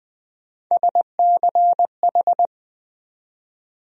Morse code, heard as a keyed tone, about 20 wpm, 700 Hz. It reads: SCH
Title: USA Call Sign Postfixes - 3 Letters 20wpm